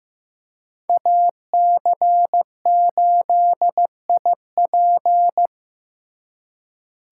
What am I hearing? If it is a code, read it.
Morse code, heard as a keyed tone, about 18 wpm, 700 Hz. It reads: AC8IP
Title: Speed Racing - USA 2x2 Call Signs 15wpm